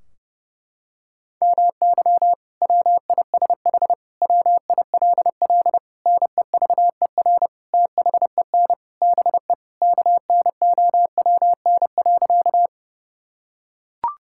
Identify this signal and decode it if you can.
Morse code, heard as a keyed tone, about 30 wpm, 700 Hz. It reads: MY WISH WILL NEVER THEN BE KNOWN.
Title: Sentences from Top 1600 Words 30wpm